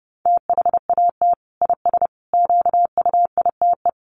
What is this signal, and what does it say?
Morse code, heard as a keyed tone, about 30 wpm, 700 Hz. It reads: THAT IS QUITE
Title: Sentences from Top 1100 Words 30wpm